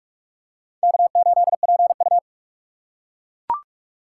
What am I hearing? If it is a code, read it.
Morse code, heard as a keyed tone, about 43 wpm, 700 Hz. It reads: K8PU
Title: USA Call Signs - 4x Character Spacing - 45wpm